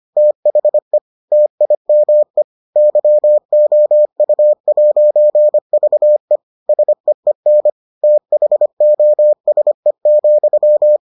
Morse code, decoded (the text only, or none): THE TIME YOU'VE SEEN THOSE,